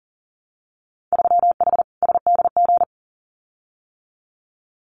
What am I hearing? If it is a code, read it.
Morse code, heard as a keyed tone, about 42 wpm, 700 Hz. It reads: 3H SDG
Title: Winter Field Day Exchanges Encore 40wpm